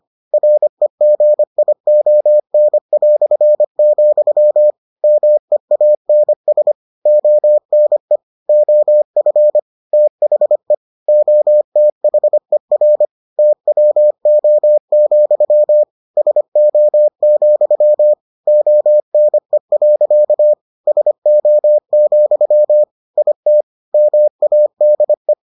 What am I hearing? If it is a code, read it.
Morse code, heard as a keyed tone, about 25 wpm, 600 Hz. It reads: REGION", MEANS ONE OF THE OTHER TWO, SO, ONE. SO, IT MADE